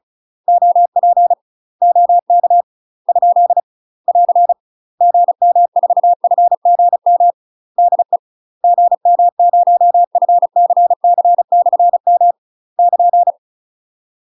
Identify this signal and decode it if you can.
Morse code, heard as a keyed tone, about 34 wpm, 700 Hz. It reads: OP OK ? <AR> GM4FGM DE GM0FCC/M <KN>
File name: UK MASTER QSOs TRAINING/35WPM